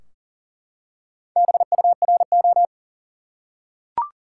Morse code, heard as a keyed tone, about 40 wpm, 700 Hz. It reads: BURO
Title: Top 90 QSO Elements 40wpm